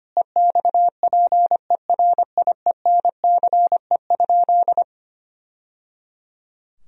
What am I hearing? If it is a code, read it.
Morse code, heard as a keyed tone, about 25 wpm, 700 Hz. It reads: EXPERIENCE?
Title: Sentences from Top 1000 Words - Review Part 4 - 25wpm